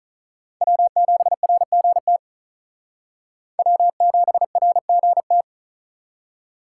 Morse code, heard as a keyed tone, about 45 wpm, 700 Hz. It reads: W7RGT W7RGT